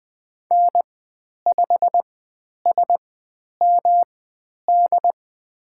Morse code, heard as a keyed tone, about 20 wpm, spaced as for 10 wpm, 700 Hz. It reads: N5SMD
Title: 109 Lesson 18g - Letter m - Call Signs 20wpm